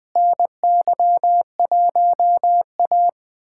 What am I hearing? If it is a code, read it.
Morse code, heard as a keyed tone, about 20 wpm, 700 Hz. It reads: NY1A
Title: USA Callsign DE Callsign - 4 Characters - 20wpm